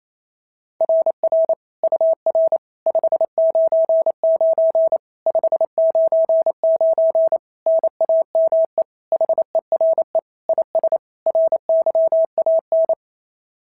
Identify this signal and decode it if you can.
Morse code, heard as a keyed tone, about 25 wpm, 650 Hz. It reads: RR UR 599 599 NAME HERE IS RYAN
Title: SOTA QSO Practice 28wpm